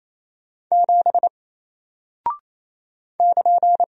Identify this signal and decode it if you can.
Morse code, heard as a keyed tone, about 28 wpm, 700 Hz. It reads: 7 <KN>
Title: Single Character (Letter:Number:Punctuation:Prosign) Alt 28wpm